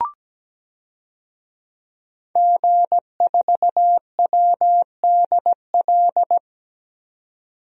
Morse code, heard as a keyed tone, about 17 wpm, 700 Hz. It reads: G4WDL
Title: DXCC-100 Call Signs - 5 Characters - 17wpm